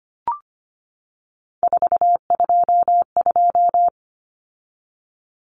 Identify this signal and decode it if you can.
Morse code, heard as a keyed tone, about 25 wpm, 700 Hz. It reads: 422
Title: Numbers - 3-Digit 25wpm